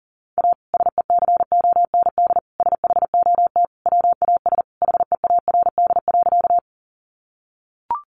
Morse code, heard as a keyed tone, about 40 wpm, 700 Hz. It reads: A SECOND SHOT WAS HEARD.
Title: Sentences from Top 1100 Words 40wpm